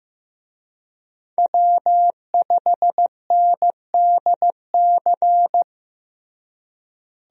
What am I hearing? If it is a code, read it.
Morse code, heard as a keyed tone, about 15 wpm, 700 Hz. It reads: W5NDC
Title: USA Call Signs - 5 Characters - Repeated 3x - 15wpm